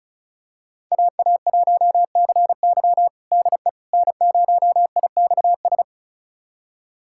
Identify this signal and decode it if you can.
Morse code, heard as a keyed tone, about 36 wpm, 700 Hz. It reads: AA1CY DE N0IXS
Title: USA Callsign DE Callsign - 5 Characters - 35wpm